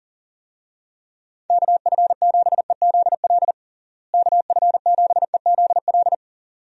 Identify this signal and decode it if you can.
Morse code, heard as a keyed tone, about 37 wpm, 700 Hz. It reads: KF7EZL KF7EZL
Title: USA Call Signs - 6 Characters - Repeated 2x - 40wpm